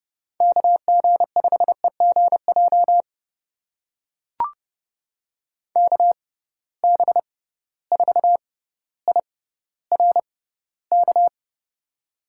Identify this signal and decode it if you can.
Morse code, heard as a keyed tone, about 30 wpm, spaced as for 15 wpm, 700 Hz. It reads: KG5EGJ KB4IRK
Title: USA Call Signs - 6 Characters - 6x Character Spacing - 30wpm